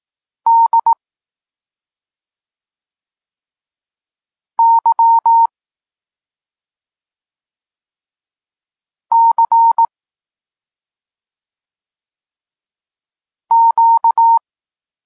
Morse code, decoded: D Y C Q